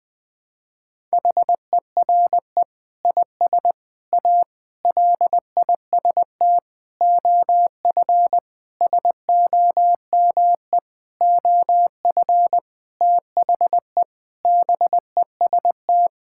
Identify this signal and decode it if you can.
Morse code, heard as a keyed tone, about 20 wpm, 700 Hz. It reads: HERE IS A LIST OF SOME OF THE BEST
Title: Sentences with 8-Word Phrases - Spoken Beginning - 20wpm